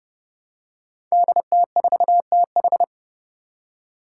Morse code, heard as a keyed tone, about 30 wpm, 700 Hz. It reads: DT4TH
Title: Five Challenging Letter-Number Combinations 30wpm